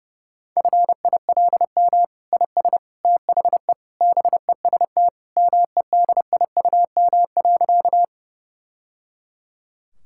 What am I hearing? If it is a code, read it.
Morse code, heard as a keyed tone, about 30 wpm, 700 Hz. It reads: FILM IS THE BEST MEDIUM.
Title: Sentences from Top 1000 Words - Review Part 6 - 30wpm